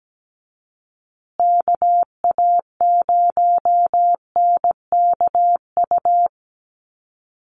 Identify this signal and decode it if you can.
Morse code, heard as a keyed tone, about 17 wpm, 700 Hz. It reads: KA0NKU
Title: USA Call Signs Repeated 3x 17wpm